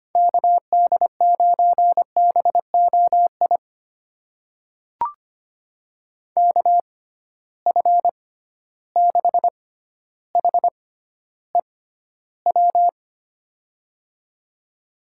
Morse code, heard as a keyed tone, about 25 wpm, spaced as for 13 wpm, 700 Hz. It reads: KD9BOI KF6HEW
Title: USA Call Signs - 6 Characters - 6x Character Spacing - 25wpm